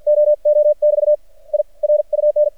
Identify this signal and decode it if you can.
Morse code, heard as a keyed tone, about 55 wpm, spaced as for 46 wpm, 600 Hz. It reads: OO <BT> I RUN
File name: archived / CRSnet